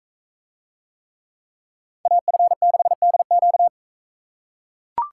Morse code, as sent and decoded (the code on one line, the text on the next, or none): .- ..-. -.... -.. --.-
AF6DQ